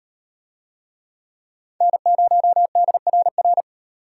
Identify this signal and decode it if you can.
Morse code, heard as a keyed tone, about 38 wpm, 700 Hz. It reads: N0DRR